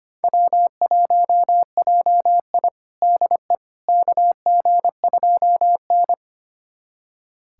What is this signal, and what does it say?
Morse code, heard as a keyed tone, about 25 wpm, 700 Hz. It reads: W1JI DE KG2N
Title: USA Callsign DE Callsign - 4 Characters - 25wpm